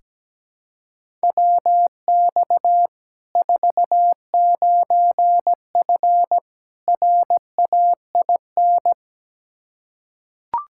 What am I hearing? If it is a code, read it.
Morse code, heard as a keyed tone, about 17 wpm, 700 Hz. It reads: WX 49F RAIN
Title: Practice QSO - Exchange 2 - Info - 17wpm